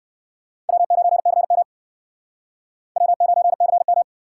Morse code, heard as a keyed tone, about 60 wpm, 700 Hz. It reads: W9CM W9CM